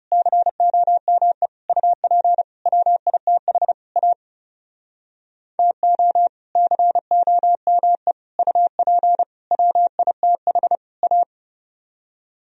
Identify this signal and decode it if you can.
Morse code, heard as a keyed tone, about 37 wpm, 700 Hz. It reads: COME UP WITH A TO COME UP WITH A
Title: Speed-Racing - Sets of 5 Words 30wpm